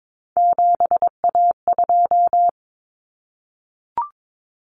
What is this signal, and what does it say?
Morse code, heard as a keyed tone, about 22 wpm, 700 Hz. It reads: 7A2
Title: Common Cut Numbers - 3-Digit 22wpm